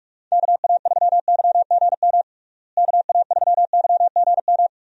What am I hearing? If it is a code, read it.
Morse code, heard as a keyed tone, about 46 wpm, 700 Hz. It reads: KA3YGM KA3YGM
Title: USA Call Signs - 6 Characters - Repeated 3x - 45wpm